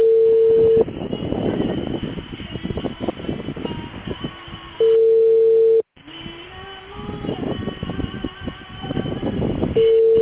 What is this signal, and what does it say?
Ringback tone (China, 450 Hz).